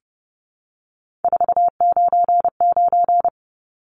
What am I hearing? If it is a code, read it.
Morse code, heard as a keyed tone, about 30 wpm, 700 Hz. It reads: 499
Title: Numbers - 3-Digit 30wpm